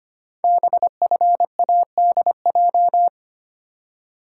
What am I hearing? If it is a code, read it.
Morse code, heard as a keyed tone, about 25 wpm, 700 Hz. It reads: BFADJ
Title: Code Groups - 5 Letters - 25wpm